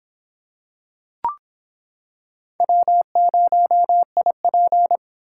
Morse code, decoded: W0IP

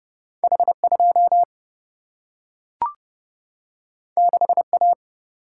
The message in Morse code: H2 6A

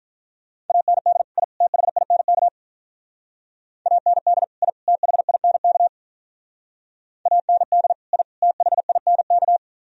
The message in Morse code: AND I THINK AND I THINK AND I THINK

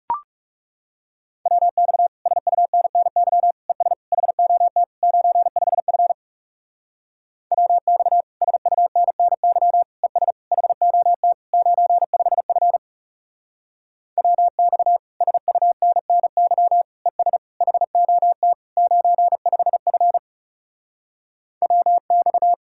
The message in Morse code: WX SUNNY ES HOT 95F WX SUNNY ES HOT 95F WX SUNNY ES HOT 95F WX

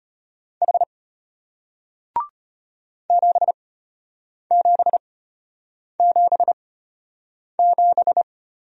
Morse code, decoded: H 7777

